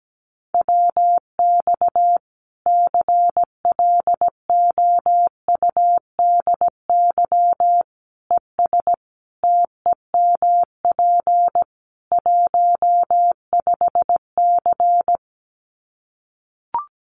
Morse code, heard as a keyed tone, about 17 wpm, 700 Hz. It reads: WX CLOUDY ES TEMP 15C
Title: Common QSO Phrases 17wpm